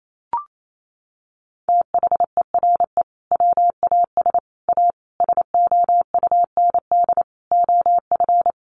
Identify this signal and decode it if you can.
Morse code, heard as a keyed tone, about 28 wpm, 700 Hz. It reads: THERE WAS A SOUND OF
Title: A Sound Of Thunder 28wpm